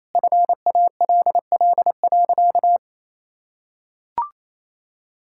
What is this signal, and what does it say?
Morse code, heard as a keyed tone, about 28 wpm, 700 Hz. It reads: FALL.
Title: Sentences from Top 1000 Words - Review Part 6 - 28wpm